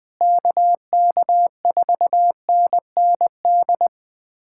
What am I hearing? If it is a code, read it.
Morse code, heard as a keyed tone, about 21 wpm, 700 Hz. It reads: KK4NND
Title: USA Callsign DE Callsign 20wpm